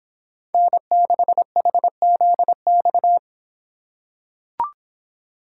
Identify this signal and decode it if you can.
Morse code, heard as a keyed tone, about 26 wpm, 700 Hz. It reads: N6HZX